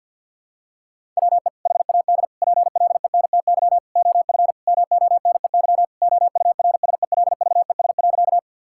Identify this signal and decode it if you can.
Morse code, heard as a keyed tone, about 48 wpm, 700 Hz. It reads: WE HAD PLENTY OF MONEY OURSELVES.
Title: Sentences from Top 1600 Words 50wpm